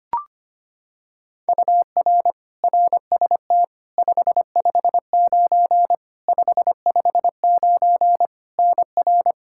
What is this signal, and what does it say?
Morse code, heard as a keyed tone, about 25 wpm, 700 Hz. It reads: UR RST 559 559 NR
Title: Practice QSO - Exchange 1 - Beginning - 25wpm